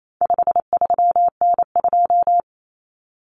Morse code, heard as a keyed tone, about 28 wpm, 700 Hz. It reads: 53N2
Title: Common Cut Numbers - 4-Digit 28wpm